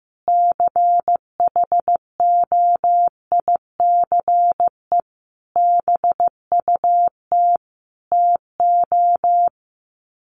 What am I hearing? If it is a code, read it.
Morse code, heard as a keyed tone, about 14 wpm, 700 Hz. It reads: CHOICE BUT TO
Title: Sets of 5 Words 15wpm